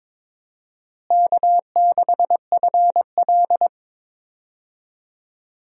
Morse code, heard as a keyed tone, about 22 wpm, 700 Hz. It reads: K6FL
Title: USA Call Signs - 4 Characters - Repeated 3x - 22wpm